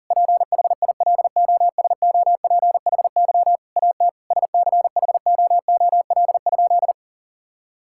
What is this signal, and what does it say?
Morse code, heard as a keyed tone, about 40 wpm, 700 Hz. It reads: PHILOSOPHY AT SCHOOL?
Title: Sentences from Top 1000 Words - Review Part 6 - 40wpm